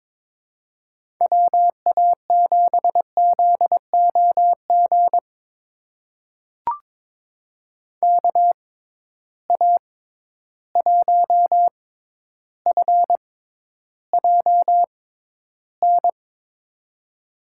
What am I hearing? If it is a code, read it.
Morse code, heard as a keyed tone, about 22 wpm, spaced as for 11 wpm, 700 Hz. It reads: WA7ZOG KA1FJN
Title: USA Call Signs - 6 Characters - 6x Character Spacing - 22wpm